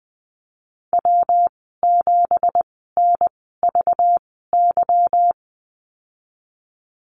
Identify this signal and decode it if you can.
Morse code, heard as a keyed tone, about 20 wpm, spaced as for 15 wpm, 700 Hz. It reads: W7NVY
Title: 231 Lesson 37g - Letter x - Call Signs 20wpm